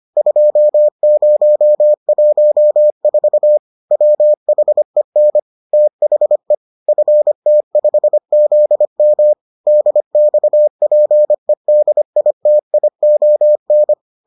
Morse code, heard as a keyed tone, about 25 wpm, 600 Hz. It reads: 2014 WHEN THE FT5ZM DXPEDITION